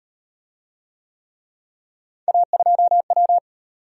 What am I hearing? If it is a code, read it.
Morse code, heard as a keyed tone, about 34 wpm, 700 Hz. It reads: A2W